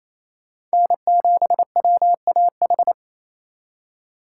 Morse code, heard as a keyed tone, about 37 wpm, 700 Hz. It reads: N7WAH